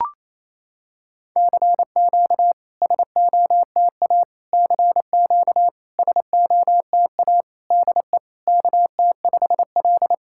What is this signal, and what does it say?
Morse code, heard as a keyed tone, about 28 wpm, 700 Hz. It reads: CQ SOTA CQ SOTA DE KT5L
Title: SOTA QSO Practice 28wpm